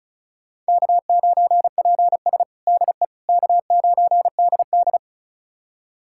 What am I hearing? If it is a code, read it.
Morse code, heard as a keyed tone, about 36 wpm, 700 Hz. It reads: K9PS DE K9DD
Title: USA Callsign DE Callsign - 4 Characters - 35wpm